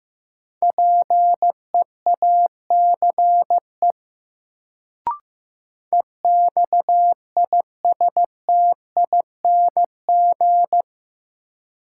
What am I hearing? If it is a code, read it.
Morse code, heard as a keyed tone, about 15 wpm, 700 Hz. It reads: PEACE EXISTING
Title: Sentences from Top 1500 Words - New Word Drill Alt - 15wpm